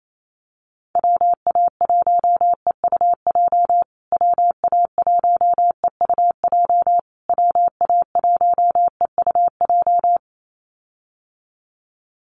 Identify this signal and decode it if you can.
Morse code, heard as a keyed tone, about 28 wpm, 700 Hz. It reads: WA1EUJ WA1EUJ WA1EUJ
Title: USA Call Signs Repeated 3x 28wpm